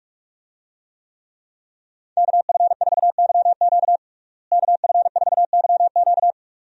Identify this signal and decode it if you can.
Morse code, heard as a keyed tone, about 42 wpm, 700 Hz. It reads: KF4YQ KF4YQ
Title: USA Call Signs - 5 Characters - Repeated 2x - 45wpm